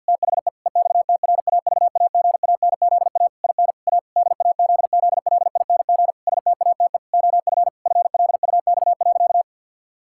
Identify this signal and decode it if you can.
Morse code, heard as a keyed tone, about 52 wpm, 700 Hz. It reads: THE EXTRAVAGANZA IN A DAZZLING STATE OF FLUX.
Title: Pangram Challenge v2 - 50wpm